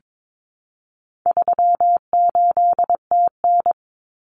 Morse code, heard as a keyed tone, about 22 wpm, 700 Hz. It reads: 38TN
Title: Common Cut Numbers - 4-Digit 22wpm